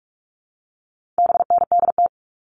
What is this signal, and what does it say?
Morse code, heard as a keyed tone, about 44 wpm, 700 Hz. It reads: BNDT
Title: All Cut Numbers - 4-Digit 45wpm